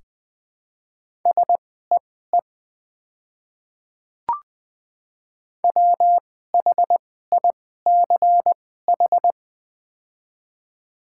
Morse code, heard as a keyed tone, about 20 wpm, spaced as for 15 wpm, 700 Hz. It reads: SEE WHICH
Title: 2x Character Spacing - Top 100 Words - 20wpm